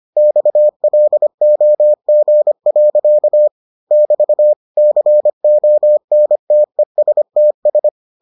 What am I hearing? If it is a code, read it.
Morse code, heard as a keyed tone, about 25 wpm, 600 Hz. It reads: XLOG. <BT> CONTESTS